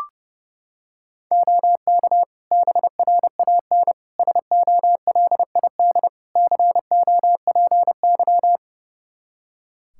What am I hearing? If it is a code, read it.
Morse code, heard as a keyed tone, about 30 wpm, 700 Hz. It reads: OK BRAN SOLID COPY
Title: Practice QSO - Exchange 3 - More Info - 30wpm